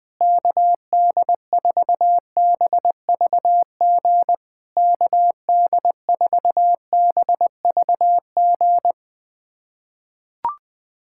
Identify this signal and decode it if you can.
Morse code, heard as a keyed tone, about 20 wpm, 690 Hz. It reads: KD4BVG KD4BVG
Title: POTA QSO Practice - 20wpm